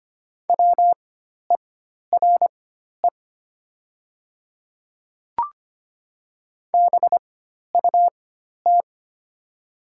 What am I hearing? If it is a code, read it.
Morse code, heard as a keyed tone, about 25 wpm, spaced as for 12 wpm, 690 Hz. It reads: WERE BUT
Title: 4x Character Spacing - Top 100 Words - 25wpm